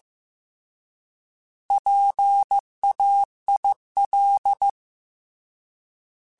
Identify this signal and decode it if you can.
Morse code, heard as a keyed tone, about 15 wpm, 800 Hz. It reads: PAIL
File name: Alphabetical / Alphabetical mp3s / Ps @13